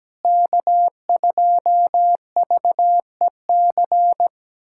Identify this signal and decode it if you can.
Morse code, heard as a keyed tone, about 17 wpm, 700 Hz. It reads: K2VEC
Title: USA Callsign DE Callsign - 5 Characters - 17wpm